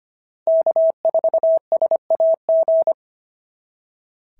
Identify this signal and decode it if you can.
Morse code, heard as a keyed tone, about 25 wpm, 690 Hz. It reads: K4SAG
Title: SOTA QSO Practice 25wpm